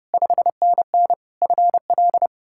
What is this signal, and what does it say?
Morse code, heard as a keyed tone, about 40 wpm, spaced as for 35 wpm, 700 Hz. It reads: 5NN FL
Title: Speed-Racing Common QSO Phrases Encore - 30wpm